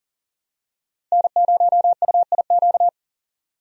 Morse code, heard as a keyed tone, about 40 wpm, 700 Hz. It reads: N0UIQ
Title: USA Call Signs - 5 Characters - 40wpm